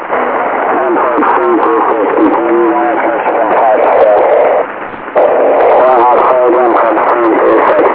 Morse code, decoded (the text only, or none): none